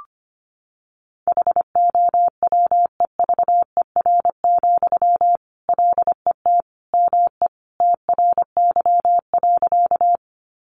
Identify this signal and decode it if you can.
Morse code, heard as a keyed tone, about 25 wpm, 700 Hz. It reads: HOWEVER, LET ME TRY.
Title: Sentences from Top 600 Words 25wpm